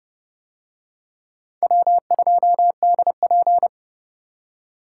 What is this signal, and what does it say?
Morse code, heard as a keyed tone, about 27 wpm, 700 Hz. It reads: W2DP